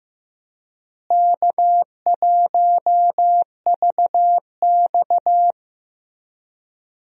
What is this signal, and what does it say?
Morse code, heard as a keyed tone, about 15 wpm, 700 Hz. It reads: K1VX
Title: USA Call Signs - 4 Characters - Repeated 3x - 15wpm